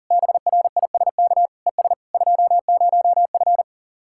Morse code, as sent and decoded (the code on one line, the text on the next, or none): -... .-. .. ... -.- / . ... / ..--- ----- ..-.
BRISK ES 20F